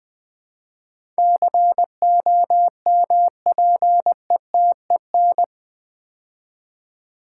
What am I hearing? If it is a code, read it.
Morse code, heard as a keyed tone, about 20 wpm, 700 Hz. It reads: COMPETEN
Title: Build A Word - 4 Letters At A Time - 10-Letter Words - 20wpm